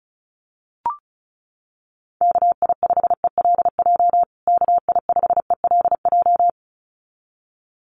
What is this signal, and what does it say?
Morse code, heard as a keyed tone, about 34 wpm, 700 Hz. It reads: KI5ELJ KI5ELJ
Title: USA Call Signs Repeated 2x 35wpm